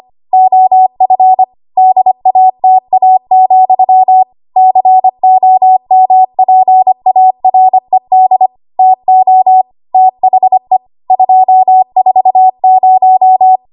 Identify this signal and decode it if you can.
Morse code, heard as a keyed tone, about 25 wpm, 750 Hz. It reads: OF DATA, COMPARED TO THE 240
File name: Archive / 25 WPM / 25WPM